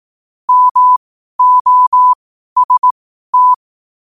Morse code, heard as a keyed tone, about 18 wpm, spaced as for 12 wpm, 1000 Hz. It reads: MOST
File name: qotdfiles / 12wpm